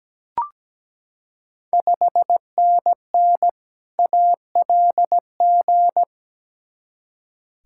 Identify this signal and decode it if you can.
Morse code, heard as a keyed tone, about 17 wpm, 690 Hz. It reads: 5NN ALG
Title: Ontario QSO Party Exchange 17wpm